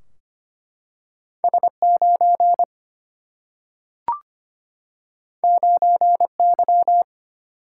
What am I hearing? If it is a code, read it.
Morse code, heard as a keyed tone, about 25 wpm, 700 Hz. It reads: S 9 9 Y
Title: Single-Letter Single-Number 25wpm